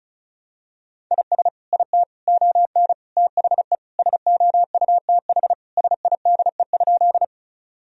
Morse code, heard as a keyed tone, about 36 wpm, 700 Hz. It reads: IS IT ON THE SOUTH SIDE?
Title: Sentences from Top 1000 Words - Review Part 2 - 35wpm